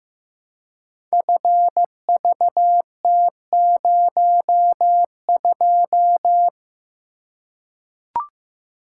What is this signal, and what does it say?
Morse code, heard as a keyed tone, about 15 wpm, 700 Hz. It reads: FVT02
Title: Code Groups - 5 Letter:Numbers - 15wpm